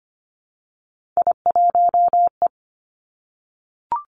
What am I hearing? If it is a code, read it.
Morse code, heard as a keyed tone, about 25 wpm, 700 Hz. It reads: I1E
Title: Three Challenging Letter-Number Combinations 25wpm